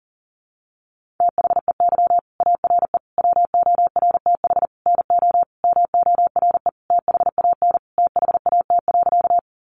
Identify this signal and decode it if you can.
Morse code, heard as a keyed tone, about 40 wpm, 700 Hz. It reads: THEY ARE WORTH NO MORE THAN THAT.
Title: Sentences from Top 800 Words 40wpm